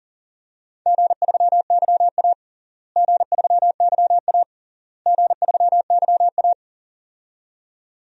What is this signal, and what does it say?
Morse code, heard as a keyed tone, about 37 wpm, 700 Hz. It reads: G3YA G3YA G3YA
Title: DXCC-100 Call Signs - 4 Characters - Repeated 3x - 40wpm